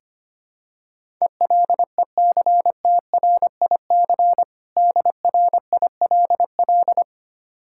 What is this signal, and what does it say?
Morse code, heard as a keyed tone, about 25 wpm, 700 Hz. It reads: ELECTRIC DRILL
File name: practice / 25wpm / Difficult Words with Hints - 25wpm